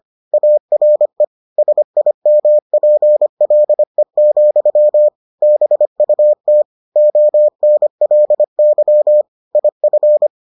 Morse code, heard as a keyed tone, about 25 wpm, 600 Hz. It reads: ARE SIMPLE, BUT ONLY IF